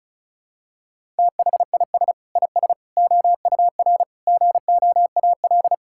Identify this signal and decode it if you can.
Morse code, heard as a keyed tone, about 36 wpm, 700 Hz. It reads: THIS IS OUR GOAL
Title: Sentences from Top 1000 Words - Review Part 5 - 35wpm